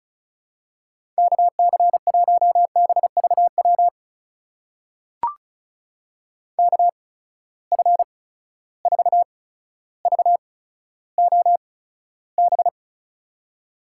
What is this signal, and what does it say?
Morse code, heard as a keyed tone, about 36 wpm, spaced as for 15 wpm, 700 Hz. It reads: KC1BVW KF4VOB
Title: USA Call Signs - 6 Characters - 8x Character Spacing - 35wpm